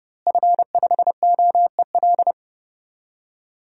Morse code, heard as a keyed tone, about 30 wpm, 700 Hz. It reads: F5OEL
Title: DXCC-100 Call Signs - 5 Characters - Repeated 2x - 30wpm